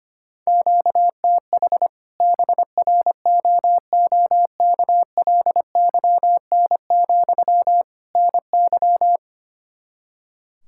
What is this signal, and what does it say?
Morse code, heard as a keyed tone, about 25 wpm, 700 Hz. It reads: QTH BROOKLYN, NY
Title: Practice QSO - Exchange 1 - Beginning - 25wpm